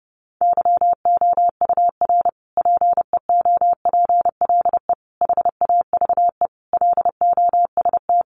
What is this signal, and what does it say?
Morse code, heard as a keyed tone, about 30 wpm, 700 Hz. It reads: YOUR PEOPLE HAVE LOST
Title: Sentences from Top 1400 Words 30wpm